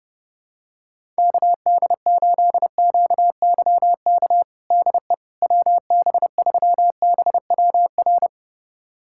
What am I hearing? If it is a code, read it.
Morse code, heard as a keyed tone, about 32 wpm, 700 Hz. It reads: KD8QYK DE WB3BWR
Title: USA Callsign DE Callsign 30wpm